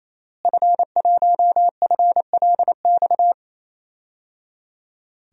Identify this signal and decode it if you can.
Morse code, heard as a keyed tone, about 28 wpm, 700 Hz. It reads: F1FLX
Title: DXCC-100 Call Signs - 5 Characters - Repeated 3x - 28wpm